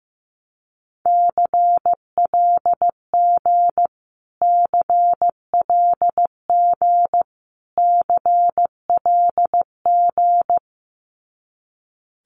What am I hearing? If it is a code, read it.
Morse code, heard as a keyed tone, about 15 wpm, 700 Hz. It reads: CLG CLG CLG
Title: Abbreviations Repeated 3x - 15wpm